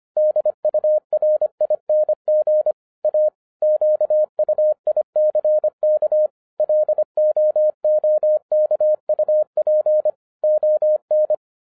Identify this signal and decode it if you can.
Morse code, heard as a keyed tone, about 25 wpm, 610 Hz. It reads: DURING A QUICK LOOKUP ON
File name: 2012-01-mp3 / n06-FIRST FISTS CONTACT-25wpm